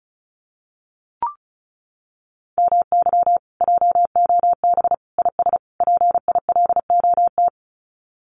Morse code, heard as a keyed tone, about 34 wpm, 700 Hz. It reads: MY JOB IS PILOT
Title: Common QSO Phrases Encore 35wpm